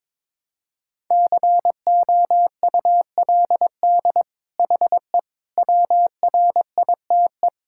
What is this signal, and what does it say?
Morse code, heard as a keyed tone, about 22 wpm, 700 Hz. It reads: COULD HE WRITE
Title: Sentences from Top 1000 Words - Review Part 6 - 22wpm